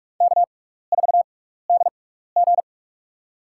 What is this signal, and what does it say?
Morse code, heard as a keyed tone, about 46 wpm, spaced as for 15 wpm, 700 Hz. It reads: K4DG
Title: USA Call Signs - 6x Character Spacing - 45wpm